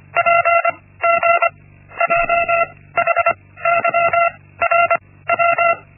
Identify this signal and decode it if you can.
Morse code, heard as a keyed tone, about 25 wpm, 680 Hz. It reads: P G J H Y R W
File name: Linksmakalnis-1